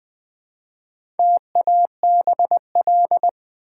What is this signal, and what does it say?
Morse code, heard as a keyed tone, about 20 wpm, 690 Hz. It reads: TABL
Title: Build A Word - 1 Letter At A Time - 5-Letter Words - 20wpm